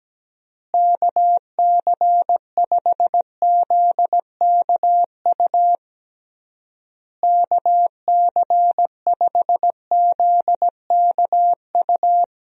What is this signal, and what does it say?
Morse code, heard as a keyed tone, about 17 wpm, 700 Hz. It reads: KC5ZKU KC5ZKU
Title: USA Call Signs - 6 Characters - Repeated 3x - 17wpm